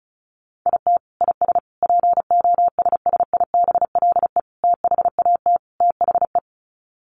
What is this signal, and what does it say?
Morse code, heard as a keyed tone, about 34 wpm, 700 Hz. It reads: IT IS POSSIBLE THAT THE
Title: Sets of 5 Words Encore2 35wpm